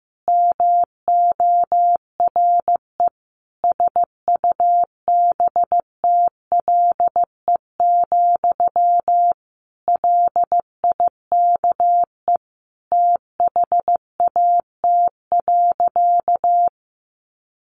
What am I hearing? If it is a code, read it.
Morse code, heard as a keyed tone, about 15 wpm, 700 Hz. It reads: MORE SUBTLE, LIKE THAT.
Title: A Sound Of Thunder 15wpm